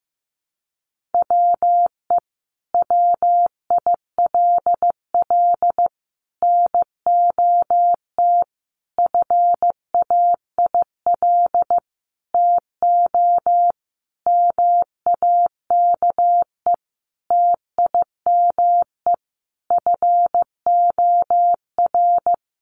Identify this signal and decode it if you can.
Morse code, heard as a keyed tone, about 15 wpm, 700 Hz. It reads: WE WILL NOT FAIL TO MAKE TIME FOR
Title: Sentences from Top 1100 Words 15wpm